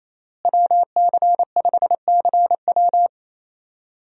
Morse code, heard as a keyed tone, about 28 wpm, 700 Hz. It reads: WC5CW
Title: USA Call Signs - 5 Characters - Repeated 3x - 28wpm